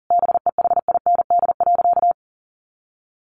Morse code, heard as a keyed tone, about 40 wpm, 700 Hz. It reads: BEHIND.
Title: Sentences from Top 700 Words 40wpm